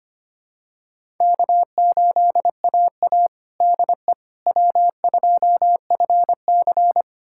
Morse code, heard as a keyed tone, about 25 wpm, 700 Hz. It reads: K8AA DE W2FC
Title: USA Callsign DE Callsign - 4 Characters - 25wpm